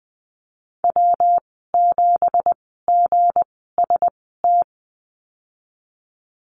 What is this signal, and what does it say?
Morse code, heard as a keyed tone, about 20 wpm, spaced as for 14 wpm, 700 Hz. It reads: W7GST
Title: 193 Lesson 31g - Letter v - Call Signs 20wpm